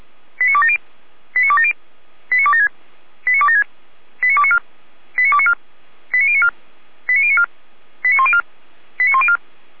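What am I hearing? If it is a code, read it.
Five-tone selective call (ZVEI), heard as quick runs of five tones, tones of about 70 ms. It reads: 89270 89270 89276 89276 89294 89294 89094 89094 89104 89104
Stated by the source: ZVEI five-tone call 89262 received on an SDR, logged by the Feuerwehr Orlamünde (fire brigade) server.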